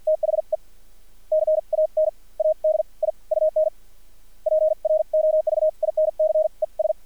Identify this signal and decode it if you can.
Morse code, heard as a keyed tone, about 50 wpm, spaced as for 37 wpm, 620 Hz. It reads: THE CANADIUN PROVINCES
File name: archived / CRSnet